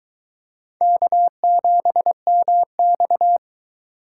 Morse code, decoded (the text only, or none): K7MX